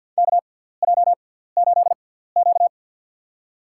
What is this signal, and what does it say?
Morse code, heard as a keyed tone, about 53 wpm, spaced as for 17 wpm, 700 Hz. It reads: KJ8Q
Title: USA Call Signs - 4 Characters - 6x Character Spacing - 50wpm